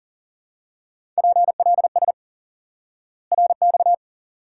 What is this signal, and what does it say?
Morse code, heard as a keyed tone, about 38 wpm, 700 Hz. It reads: PLS RX
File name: IQE Course 3 - Lesson 89 - YAGI - Two Elements - 40wpm